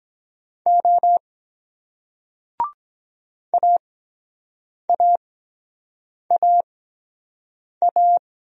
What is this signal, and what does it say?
Morse code, heard as a keyed tone, about 23 wpm, spaced as for 7 wpm, 700 Hz. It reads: O AAAA